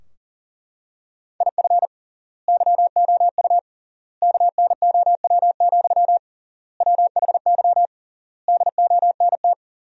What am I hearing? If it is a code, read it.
Morse code, heard as a keyed tone, about 40 wpm, 700 Hz. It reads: IF YOU KNOW, WHY DONT
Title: 3x Word-Spacing - Sentences from Top 300 Words - 40wpm